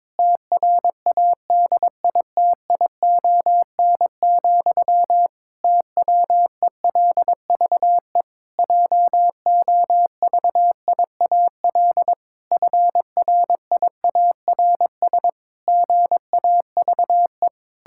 Morse code, heard as a keyed tone, about 22 wpm, 700 Hz. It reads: TRADITION, TWELVE JOVIAL FRIARS GAVE
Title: Pangram Challenge v2 - 22wpm